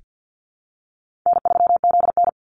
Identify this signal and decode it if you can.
Morse code, heard as a keyed tone, about 48 wpm, 700 Hz. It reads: N3ZN
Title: Easy Call Signs - 50wpm